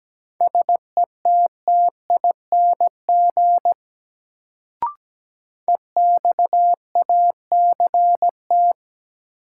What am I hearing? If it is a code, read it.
Morse code, heard as a keyed tone, about 17 wpm, 700 Hz. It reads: SETTING EXACT
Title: Sentences from Top 1300 Words - New Word Drill Alt - 17wpm